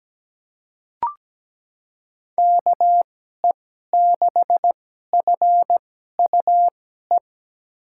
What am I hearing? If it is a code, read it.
Morse code, heard as a keyed tone, about 17 wpm, spaced as for 12 wpm, 700 Hz. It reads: KE6FUE
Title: USA Call Signs - 6 Characters - 2x Character Spacing - 17wpm